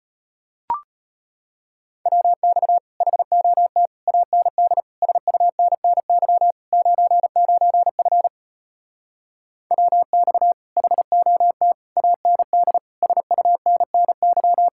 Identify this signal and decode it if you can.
Morse code, heard as a keyed tone, about 33 wpm, 700 Hz. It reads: WX HOT AND SUNNY 99F WX HOT AND SUNNY